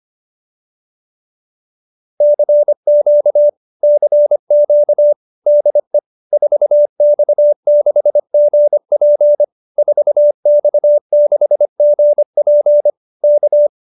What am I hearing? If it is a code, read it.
Morse code, heard as a keyed tone, about 25 wpm, 590 Hz. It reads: CQ CQ DE 4X6GP 4X6GP K